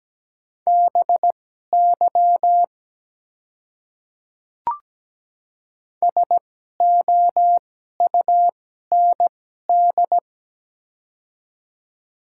Morse code, heard as a keyed tone, about 17 wpm, spaced as for 12 wpm, 700 Hz. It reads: BY SOUND
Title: 2x Character Spacing - Top 200 Words - 17wpm